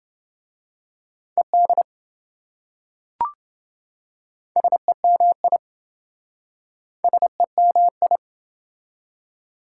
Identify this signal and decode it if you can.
Morse code, heard as a keyed tone, about 26 wpm, 700 Hz. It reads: ED SEMI SEMI